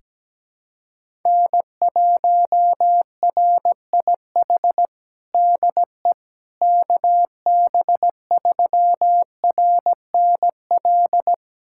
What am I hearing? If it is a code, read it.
Morse code, heard as a keyed tone, about 17 wpm, 700 Hz. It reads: N1RIH DE KB3RNL
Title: USA Callsign DE Callsign 17wpm